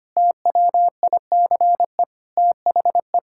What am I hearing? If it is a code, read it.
Morse code, heard as a keyed tone, about 25 wpm, 700 Hz. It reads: TWICE THE
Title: Sentences from Top 1600 Words 25wpm